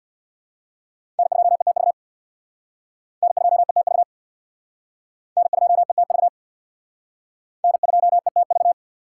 Morse code, heard as a keyed tone, about 64 wpm, 700 Hz. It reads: N2ETV N2ETV N2ETV N2ETV